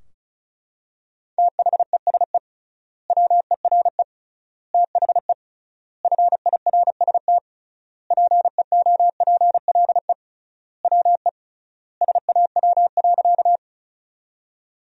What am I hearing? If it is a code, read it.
Morse code, heard as a keyed tone, about 36 wpm, 700 Hz. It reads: THESE WERE THE FIRST PEOPLE WE SAW.
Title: 3x Word-Spacing - Sentences from Top 300 Words - 35wpm